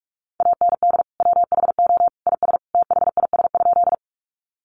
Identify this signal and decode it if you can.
Morse code, heard as a keyed tone, about 44 wpm, 700 Hz. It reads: AND WHO IS THIS?
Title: Sentences from Top 100 Words 45wpm